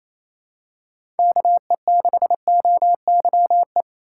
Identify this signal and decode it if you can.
Morse code, heard as a keyed tone, about 28 wpm, 700 Hz. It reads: KE6OYE
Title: USA Call Signs - 6 Characters - 2x Character Spacing - 28wpm